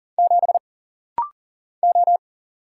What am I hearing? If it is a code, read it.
Morse code, heard as a keyed tone, about 40 wpm, 700 Hz. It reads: 7 O
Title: Single Letter-Number - Rapid-Fire 40wpm